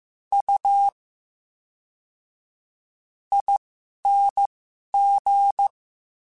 Morse code, heard as a keyed tone, about 15 wpm, spaced as for 11 wpm, 800 Hz. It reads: U ING
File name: Special INGs / Special INGs-mp3s / VerySlowToFaster Ing @10